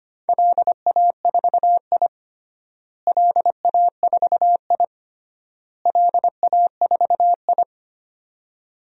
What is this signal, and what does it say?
Morse code, heard as a keyed tone, about 25 wpm, 700 Hz. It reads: LA4I LA4I LA4I
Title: DXCC-100 Call Signs - 4 Characters - Repeated 3x - 25wpm